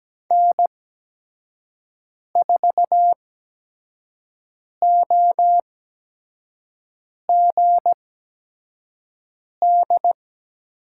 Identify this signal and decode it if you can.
Morse code, heard as a keyed tone, about 17 wpm, spaced as for 5 wpm, 700 Hz. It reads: N4OGD
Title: USA Call Signs - 8x Character Spacing - 17wpm